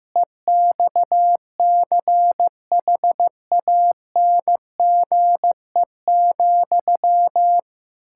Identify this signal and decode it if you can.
Morse code, heard as a keyed tone, about 15 wpm, 700 Hz. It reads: EXCHANGE,
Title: Sentences from Top 1000 Words - Review Part 6 - 15wpm